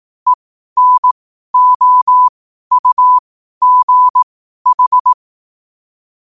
Morse code, heard as a keyed tone, about 18 wpm, spaced as for 12 wpm, 1000 Hz. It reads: ENOUGH
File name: qotdfiles / 12wpm